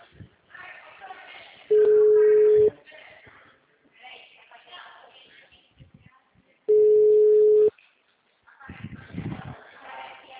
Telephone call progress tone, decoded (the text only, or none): ringback tone